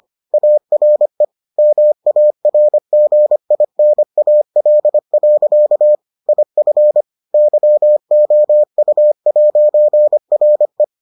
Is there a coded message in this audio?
Yes: Morse code from 0.3 to 10.8 s, heard as a keyed tone, about 25 wpm, 600 Hz. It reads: ARE MARGINAL. IF YOU'RE